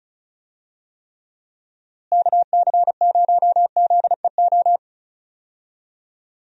Morse code, heard as a keyed tone, about 34 wpm, 700 Hz. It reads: KC0ZEO
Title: USA Call Signs - 6 Characters - Repeated 2x - 35wpm